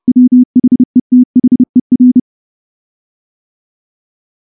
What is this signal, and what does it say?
Morse code, heard as a keyed tone, about 28 wpm, 250 Hz. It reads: WHETHER
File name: morse-training-30WPM